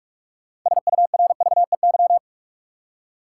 Morse code, heard as a keyed tone, about 46 wpm, 700 Hz. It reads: SURVEY
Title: Six Letter Words - 45wpm